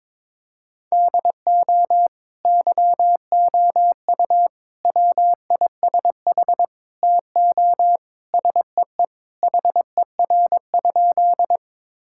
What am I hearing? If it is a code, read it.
Morse code, heard as a keyed tone, about 22 wpm, 700 Hz. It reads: DO YOU WISH TO SEE HER?
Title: Sentences from Top 1600 Words 22wpm